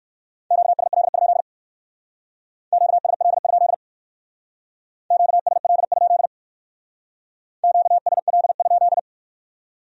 Morse code, heard as a keyed tone, about 55 wpm, 700 Hz. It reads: QSL? QSL? QSL? QSL?
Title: Speed-Racing Q-Codes - 45wpm